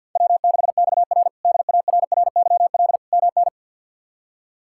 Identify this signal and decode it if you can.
Morse code, heard as a keyed tone, about 52 wpm, 700 Hz. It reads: W6XR DARRYL MN
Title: K1USN SST QSO Practice - 50wpm